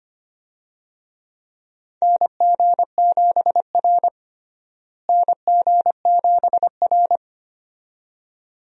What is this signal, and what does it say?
Morse code, heard as a keyed tone, about 25 wpm, 700 Hz. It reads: NG7R NG7R
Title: USA Call Signs - 4 Characters - Repeated 2x - 25wpm